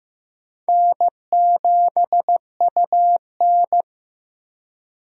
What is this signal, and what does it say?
Morse code, heard as a keyed tone, about 15 wpm, 700 Hz. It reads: N7UN
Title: USA Call Signs - 4 Characters - 15wpm